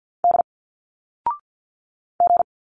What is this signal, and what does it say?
Morse code, heard as a keyed tone, about 50 wpm, 700 Hz. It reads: D G
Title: Single Letters 50wpm